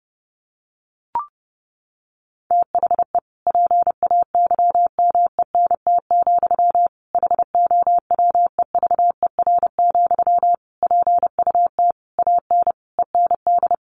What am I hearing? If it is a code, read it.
Morse code, heard as a keyed tone, about 29 wpm, 700 Hz. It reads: THE PAYMENT, HOWEVER, PUT AN END
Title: Sentences from Top 1200 Words 30wpm